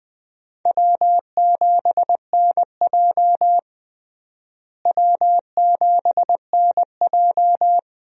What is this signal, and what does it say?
Morse code, heard as a keyed tone, about 20 wpm, 700 Hz. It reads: W7NJ W7NJ
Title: USA Call Signs - 4 Characters - Repeated 2x - 20wpm